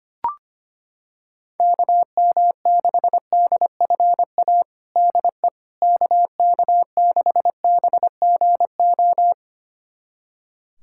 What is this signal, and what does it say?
Morse code, heard as a keyed tone, about 25 wpm, 700 Hz. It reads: KM6DFA DE KK6BGO
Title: USA Callsign DE Callsign - 6 Characters - 25wpm